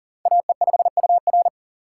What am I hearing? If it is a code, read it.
Morse code, heard as a keyed tone, about 40 wpm, 700 Hz. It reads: AE5UR
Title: USA Call Signs - 5 Characters - 3x Character Spacing - 40wpm